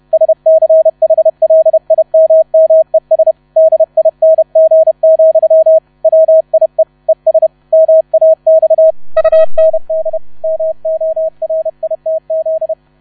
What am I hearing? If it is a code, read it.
Morse code, heard as a keyed tone, about 31 wpm, 640 Hz. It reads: SCHLIMMES DING, WIE ES MAX UND MORITZ
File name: 30wpm